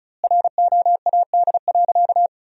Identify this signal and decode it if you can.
Morse code, heard as a keyed tone, about 36 wpm, 700 Hz. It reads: ROAD.
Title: 2x Word-Spacing - Sentences from Top 400 Words - 35wpm